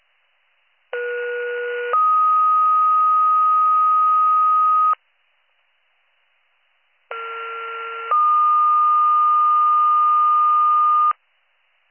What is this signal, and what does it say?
Two-tone page, heard as a one-second tone, then a three-second tone, about 480 then 1250 Hz, about 1370 then 1220 Hz. It is heard from 0.9 to 11.1 s.